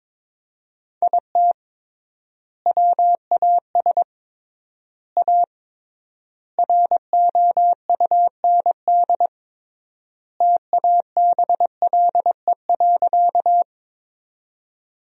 Morse code, heard as a keyed tone, about 22 wpm, 700 Hz. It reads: IT WAS A ROUND TABLE.
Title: 3x Word-Spacing - Sentences from Top 400 Words - 22wpm